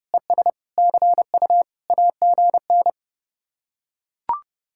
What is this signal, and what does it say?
Morse code, heard as a keyed tone, about 30 wpm, 700 Hz. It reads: ES CU AGN